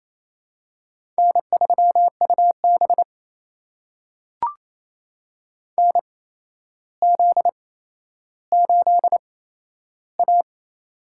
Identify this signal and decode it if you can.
Morse code, heard as a keyed tone, about 27 wpm, spaced as for 12 wpm, 700 Hz. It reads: N3UB NZ8A
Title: USA Call Signs - 4 Characters - 8x Character Spacing - 28wpm